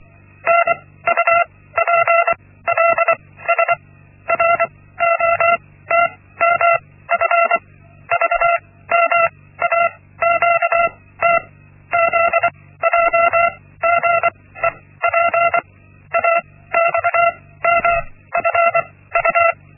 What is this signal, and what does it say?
Morse code, heard as a keyed tone, about 24 wpm, spaced as for 16 wpm, 680 Hz. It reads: NUPLS ROTMF VMAQT ZJGEP AXMFU
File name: Linksmakalnis-1